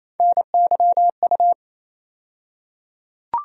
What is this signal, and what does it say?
Morse code, heard as a keyed tone, about 28 wpm, 700 Hz. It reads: NYU
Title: USA Call Sign Postfixes - 3 Letters 28wpm